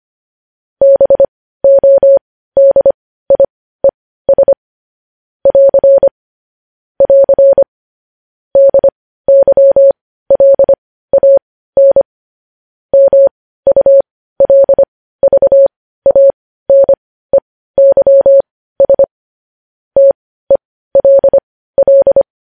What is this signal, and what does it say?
Morse code, heard as a keyed tone, about 25 wpm, spaced as for 15 wpm, 550 Hz. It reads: BODIES <AR> <AR> DYLAN MULVANEYS TELL
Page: News Headlines in Morse Code at 15 WPM